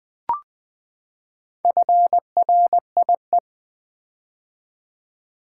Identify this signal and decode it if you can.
Morse code, heard as a keyed tone, about 20 wpm, 700 Hz. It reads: FRIE
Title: Build A Word - 4 Letters At A Time - 10-Letter Words - 20wpm